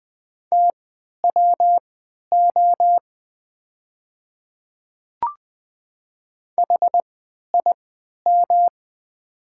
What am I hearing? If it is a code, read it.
Morse code, heard as a keyed tone, about 20 wpm, spaced as for 11 wpm, 700 Hz. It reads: TWO HIM
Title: 3x Character Spacing - Top 300 Words - 20wpm